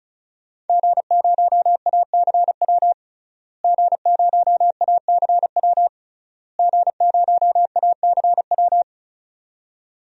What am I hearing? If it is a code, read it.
Morse code, heard as a keyed tone, about 36 wpm, 700 Hz. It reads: G0ACW G0ACW G0ACW
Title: DXCC-100 Call Signs - 5 Characters - Repeated 3x - 35wpm